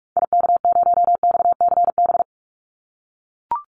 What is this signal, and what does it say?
Morse code, heard as a keyed tone, about 44 wpm, 700 Hz. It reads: IK0XCB
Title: Call Signs - 45wpm